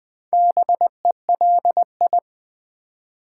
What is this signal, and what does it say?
Morse code, heard as a keyed tone, about 20 wpm, 700 Hz. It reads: BELI
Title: Build A Word - 2 Letters At A Time - 6-Letter Words - 20wpm